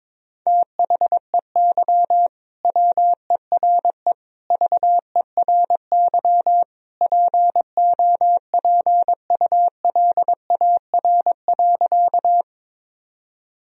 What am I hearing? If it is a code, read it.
Morse code, heard as a keyed tone, about 22 wpm, 700 Hz. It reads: THEY WERE VERY POPULAR.
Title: Sentences from Top 1000 Words - Review Part 3 - 22wpm